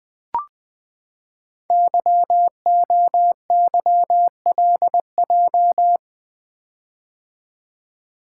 Morse code, heard as a keyed tone, about 20 wpm, 700 Hz. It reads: YOYLJ
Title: Code Groups - 5 Letter:Numbers - 20wpm